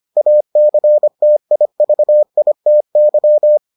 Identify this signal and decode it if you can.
Morse code, heard as a keyed tone, about 25 wpm, 600 Hz. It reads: ACTIVITY